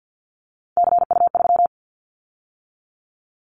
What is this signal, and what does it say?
Morse code, heard as a keyed tone, about 50 wpm, 700 Hz. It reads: CU3
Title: Callsign Prefixes for Top 50 DXCC - 50wpm